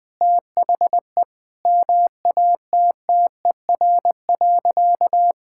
Morse code, heard as a keyed tone, about 20 wpm, 700 Hz. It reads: THE MATTER.
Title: Sentences from Top 1000 Words - Review Part 6 - 20wpm